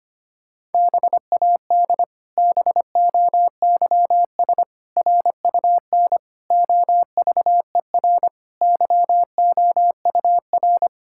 Morse code, heard as a keyed tone, about 25 wpm, 700 Hz. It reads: BAD BOYS RUN OVER YOUR
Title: Pangram Challenge v2 - 25wpm